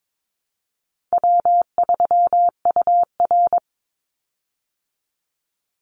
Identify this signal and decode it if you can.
Morse code, heard as a keyed tone, about 22 wpm, 700 Hz. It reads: W3UR
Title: Call Signs Repeated 3x - 22wpm